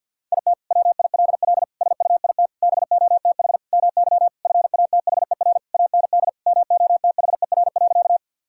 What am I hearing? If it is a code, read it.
Morse code, heard as a keyed tone, about 48 wpm, 700 Hz. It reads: IT WILL SUIT BOTH MY FATHER AND MOTHER.
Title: Sentences from Top 1600 Words 50wpm